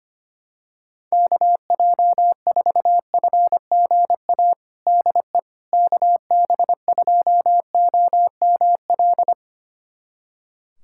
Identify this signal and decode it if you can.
Morse code, heard as a keyed tone, about 25 wpm, 700 Hz. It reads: KJ4FGA DE KB2OML
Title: USA Callsign DE Callsign - 6 Characters - 25wpm